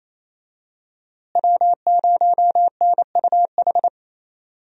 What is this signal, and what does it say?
Morse code, heard as a keyed tone, about 27 wpm, 700 Hz. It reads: W0NUH
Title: USA Call Signs - 5 Characters - Repeated 2x - 28wpm